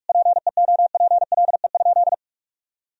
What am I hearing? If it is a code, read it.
Morse code, heard as a keyed tone, about 41 wpm, 700 Hz. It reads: PEOPLE?
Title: Sentences from Top 200 Words and Abbreviations 45wpm